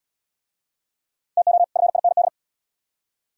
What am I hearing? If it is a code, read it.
Morse code, heard as a keyed tone, about 65 wpm, 700 Hz. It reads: TO FIND